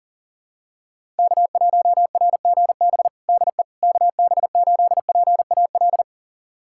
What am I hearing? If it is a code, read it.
Morse code, heard as a keyed tone, about 40 wpm, 700 Hz. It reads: K1RGB DE KB8PAL
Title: USA Callsign DE Callsign 40wpm